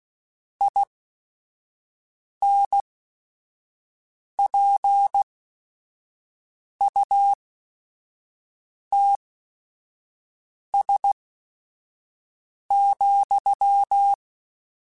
Morse code, heard as a keyed tone, about 16 wpm, spaced as for 5 wpm, 800 Hz. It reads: INPUTS,
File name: Technical Subjects